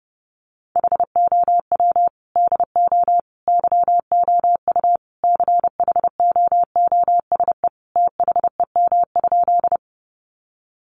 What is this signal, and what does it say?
Morse code, heard as a keyed tone, about 30 wpm, 700 Hz. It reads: HOW DO YOU CHOOSE THEM?
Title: Sentences from Top 700 Words 30wpm